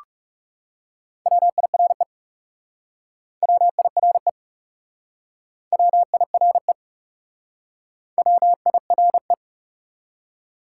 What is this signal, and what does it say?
Morse code, heard as a keyed tone, about 37 wpm, 700 Hz. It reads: WIRE WIRE WIRE WIRE
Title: Speed-Racing Top 180 QSO Elements 30wpm